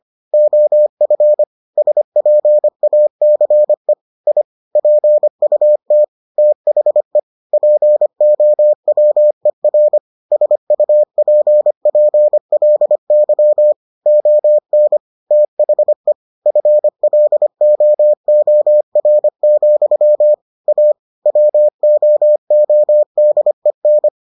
Morse code, heard as a keyed tone, about 25 wpm, 600 Hz. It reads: OF SPACE I PUT THE POWER SUPPLY ON THE FLOOR, A WOODEN